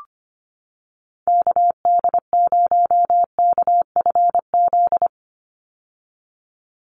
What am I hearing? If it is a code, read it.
Morse code, heard as a keyed tone, about 25 wpm, 700 Hz. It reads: KD0KFZ
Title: USA Call Signs 25wpm